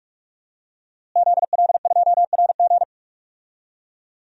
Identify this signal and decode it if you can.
Morse code, heard as a keyed tone, about 46 wpm, 700 Hz. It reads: ZL2RG
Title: DXCC-100 Call Signs - 5 Characters - Repeated 3x - 45wpm